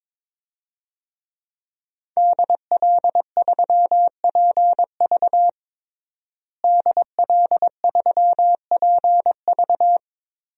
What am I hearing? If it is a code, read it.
Morse code, heard as a keyed tone, about 22 wpm, 700 Hz. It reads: DL3PV DL3PV
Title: DXCC-100 Call Signs - 5 Characters - Repeated 2x - 22wpm